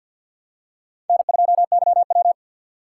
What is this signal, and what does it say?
Morse code, heard as a keyed tone, about 52 wpm, 700 Hz. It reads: N2YW
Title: USA Call Signs - 6x Character Spacing - 50wpm